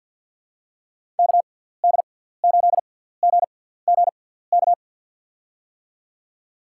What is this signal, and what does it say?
Morse code, heard as a keyed tone, about 52 wpm, spaced as for 17 wpm, 700 Hz. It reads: KD8GGK
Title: USA Call Signs - 6 Characters - 6x Character Spacing - 50wpm